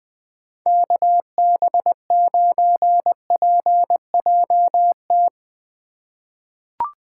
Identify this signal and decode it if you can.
Morse code, heard as a keyed tone, about 21 wpm, 700 Hz. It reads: KB9PJT
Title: USA Call Signs - 6 Characters - 2x Character Spacing - 20wpm